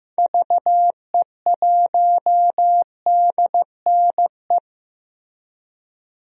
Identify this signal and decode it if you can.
Morse code, heard as a keyed tone, about 15 wpm, 700 Hz. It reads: VE1DNE
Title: DXCC-100 Call Signs - 6 Characters - Repeated 2x - 15wpm